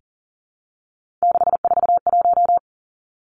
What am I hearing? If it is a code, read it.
Morse code, heard as a keyed tone, about 41 wpm, 700 Hz. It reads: 641
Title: Three Challenging Letter-Number Combinations 40wpm